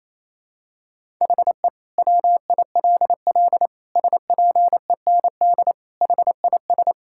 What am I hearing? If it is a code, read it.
Morse code, heard as a keyed tone, about 28 wpm, 700 Hz. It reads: HE WILL SPEND HIS
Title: Sentences from Top 1000 Words - Review Part 5 - 28wpm